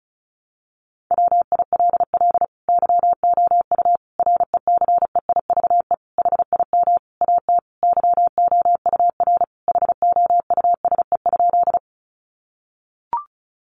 Morse code, heard as a keyed tone, about 34 wpm, 700 Hz. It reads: WILL YOU RECEIVE HIM AT YOUR HOUSE?
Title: Sentences from Top 1000 Words 35wpm